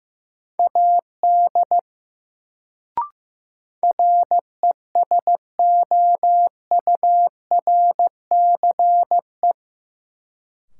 Morse code, heard as a keyed tone, about 15 wpm, 700 Hz. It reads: AD RESOURCE
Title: Sentences from Top 1300 Words - New Word Drill Alt - 15wpm